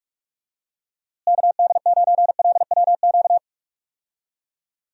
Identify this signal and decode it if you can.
Morse code, heard as a keyed tone, about 46 wpm, 700 Hz. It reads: KD9LWQ
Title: USA Call Signs - 6 Characters - Repeated 3x - 45wpm